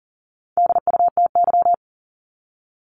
Morse code, heard as a keyed tone, about 40 wpm, 700 Hz. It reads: DUTY
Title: Top 1251-1500 Words 40wpm